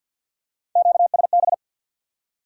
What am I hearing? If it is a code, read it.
Morse code, heard as a keyed tone, about 54 wpm, 700 Hz. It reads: QSB
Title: Top 90 QSO Elements 50wpm